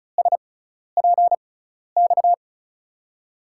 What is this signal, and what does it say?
Morse code, heard as a keyed tone, about 34 wpm, spaced as for 12 wpm, 700 Hz. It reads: SPX
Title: USA Call Signs - 6 Characters - 6x Character Spacing - 35wpm